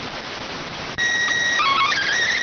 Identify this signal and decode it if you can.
SSTV (slow-scan television), PD 290.